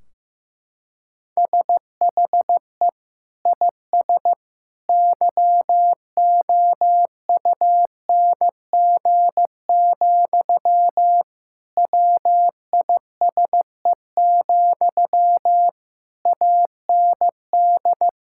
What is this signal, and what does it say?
Morse code, heard as a keyed tone, about 16 wpm, 700 Hz. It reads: SHE IS YOUNG, WISE, AND
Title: Sentences from Top 1600 Words 15wpm